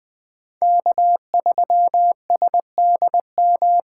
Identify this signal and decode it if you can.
Morse code, heard as a keyed tone, about 20 wpm, 700 Hz. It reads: K3SDM
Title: 2023 POTA Hunter Call Signs - 20wpm